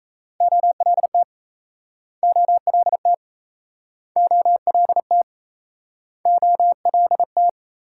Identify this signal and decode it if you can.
Morse code, heard as a keyed tone, about 37 wpm, 700 Hz. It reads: OLT OLT OLT OLT